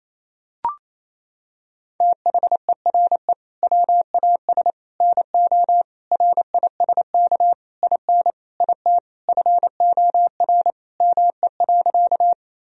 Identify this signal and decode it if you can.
Morse code, heard as a keyed tone, about 28 wpm, 700 Hz. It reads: THERE WAS NO RISK IN IT FOR ME.
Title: Sentences from Top 1000 Words - Review Part 4 - 28wpm